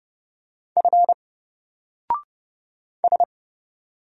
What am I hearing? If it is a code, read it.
Morse code, heard as a keyed tone, about 30 wpm, 700 Hz. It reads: F S
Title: Single Letter-Number 30wpm